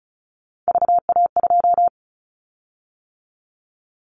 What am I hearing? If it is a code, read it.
Morse code, heard as a keyed tone, about 34 wpm, 700 Hz. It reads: VA2
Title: Callsign Prefixes for Top 100 DXCC - 35wpm